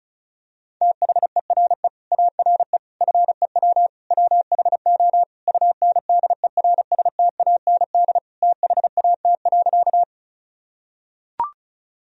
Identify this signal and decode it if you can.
Morse code, heard as a keyed tone, about 36 wpm, 700 Hz. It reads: THERE ARE FEW WHO UNDERSTAND THAT.
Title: Sentences from Top 1000 Words - Review Part 1 - 35wpm